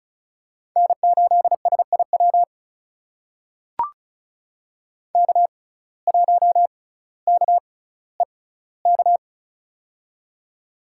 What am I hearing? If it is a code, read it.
Morse code, heard as a keyed tone, about 36 wpm, spaced as for 18 wpm, 700 Hz. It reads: N8SIW K1KEK
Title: USA Call Signs - 5 Characters - 6x Character Spacing - 35wpm